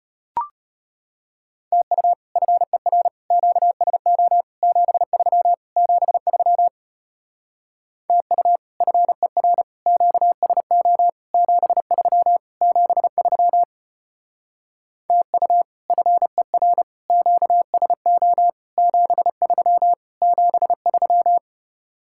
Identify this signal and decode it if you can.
Morse code, heard as a keyed tone, about 33 wpm, 700 Hz. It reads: TU FER QSO 73 73 TU FER QSO 73 73 TU FER QSO 73 73